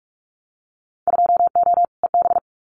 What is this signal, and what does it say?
Morse code, heard as a keyed tone, about 46 wpm, 700 Hz. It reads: 2O EB
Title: Winter Field Day Exchanges Encore 45wpm